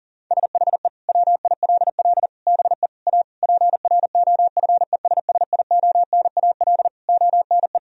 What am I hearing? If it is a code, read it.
Morse code, heard as a keyed tone, about 37 wpm, 700 Hz. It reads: SHE WILL BE A PROFESSIONAL ONE
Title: Sentences from Top 1000 Words - Review Part 4 - 40wpm